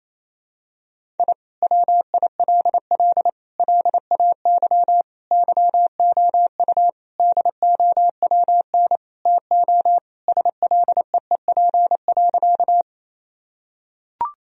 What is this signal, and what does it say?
Morse code, heard as a keyed tone, about 28 wpm, 700 Hz. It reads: I WILL LAY YOU DOWN TO SLEEP.
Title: Sentences from Top 1000 Words - Review Part 2 - 28wpm